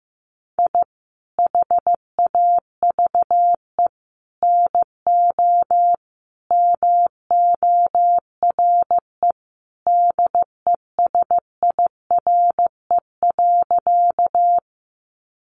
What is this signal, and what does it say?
Morse code, heard as a keyed tone, about 15 wpm, 700 Hz. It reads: I HAVE NO MORE DESIRE.
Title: Sentences from Top 1100 Words 15wpm